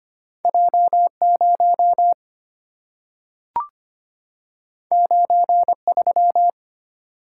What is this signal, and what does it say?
Morse code, heard as a keyed tone, about 25 wpm, 700 Hz. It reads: J0 93
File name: Two Letter-Number Combinations - 25wpm